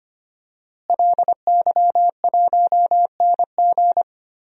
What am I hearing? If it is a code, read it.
Morse code, heard as a keyed tone, about 25 wpm, 700 Hz. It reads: LY1NG
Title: DXCC-100 Call Signs - 5 Characters - Repeated 3x - 25wpm